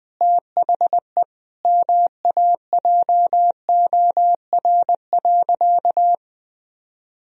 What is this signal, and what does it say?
Morse code, heard as a keyed tone, about 20 wpm, 700 Hz. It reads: THE MAJOR.
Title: Sentences from Top 1000 Words - Review Part 4 - 20wpm